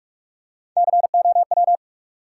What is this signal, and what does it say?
Morse code, heard as a keyed tone, about 47 wpm, 700 Hz. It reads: COW
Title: Three Letter Words - 45wpm